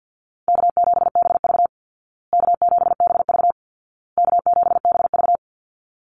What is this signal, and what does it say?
Morse code, heard as a keyed tone, about 50 wpm, 700 Hz. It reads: K7BV K7BV K7BV
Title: Easy Call Signs Repeated 3x - 50wpm